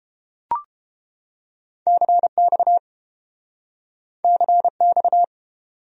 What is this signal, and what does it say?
Morse code, heard as a keyed tone, about 30 wpm, 700 Hz. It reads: CX CX